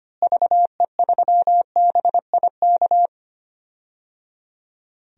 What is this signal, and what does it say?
Morse code, heard as a keyed tone, about 25 wpm, 700 Hz. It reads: VE3BIK
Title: DXCC-100 Call Signs - 6 Characters - Repeated 3x - 25wpm